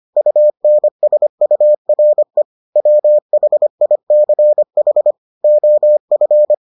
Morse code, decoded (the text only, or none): UNSURE WHICH OF